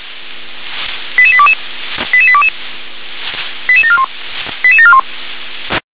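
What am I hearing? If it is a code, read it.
Five-tone selective call (ZVEI), heard as quick runs of five tones, tones of about 70 ms. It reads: 88822 88822 88741 88741
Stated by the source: ZVEI five-tone selective call received by SDR.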